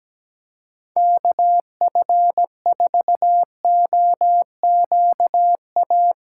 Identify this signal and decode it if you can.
Morse code, heard as a keyed tone, about 17 wpm, 700 Hz. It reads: KF4OQA
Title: USA Callsign DE Callsign - 6 Characters - 17wpm